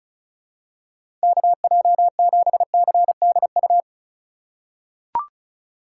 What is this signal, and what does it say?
Morse code, heard as a keyed tone, about 34 wpm, 700 Hz. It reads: KJ7CDU
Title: USA Call Signs - 2x Character Spacing - 35wpm